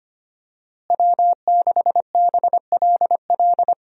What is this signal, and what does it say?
Morse code, heard as a keyed tone, about 25 wpm, 700 Hz. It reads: W6BLL
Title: USA Call Signs - 5 Characters - Repeated 2x - 25wpm